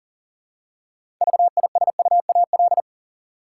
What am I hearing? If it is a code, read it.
Morse code, heard as a keyed tone, about 40 wpm, 700 Hz. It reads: VISUAL
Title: Six Letter Words - 40wpm